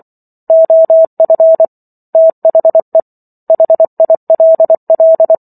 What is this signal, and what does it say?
Morse code, heard as a keyed tone, about 24 wpm, 650 Hz. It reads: OF THE HILL
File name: Chapter 15 - 24 WPM